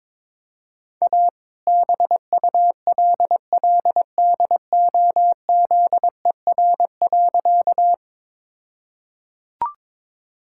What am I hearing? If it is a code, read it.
Morse code, heard as a keyed tone, about 22 wpm, 700 Hz. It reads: A BULLDOZER.
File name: All-Age Appropriate Jokes - 22wpm